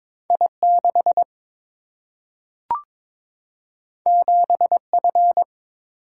Morse code, heard as a keyed tone, about 22 wpm, 700 Hz. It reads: I 6 7 F
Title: Single-Letter Single-Number 22wpm